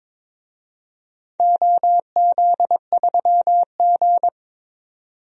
Morse code, heard as a keyed tone, about 22 wpm, 700 Hz. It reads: OZ3G
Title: DXCC-100 Call Signs - 4 Characters - 22wpm